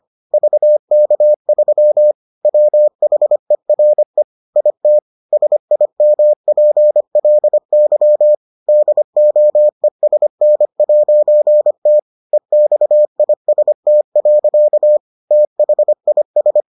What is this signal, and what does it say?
Morse code, heard as a keyed tone, about 25 wpm, 600 Hz. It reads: VK3 WHERE IT SIMPLY DOESN'T EXIST. THIS